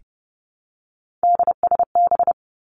Morse code, heard as a keyed tone, about 30 wpm, 700 Hz. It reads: DSB
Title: Three Challenging Letter-Number Combinations 30wpm